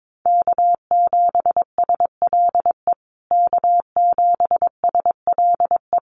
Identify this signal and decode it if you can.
Morse code, heard as a keyed tone, about 22 wpm, 700 Hz. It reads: K7SLE K7SLE
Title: USA Call Signs Repeated 3x 22wpm